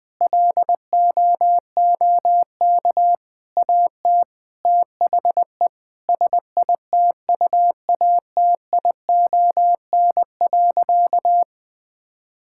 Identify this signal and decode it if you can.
Morse code, heard as a keyed tone, about 20 wpm, 700 Hz. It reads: LOOK AT THE SITUATION.
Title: Sentences with 8-Word Phrases - 20wpm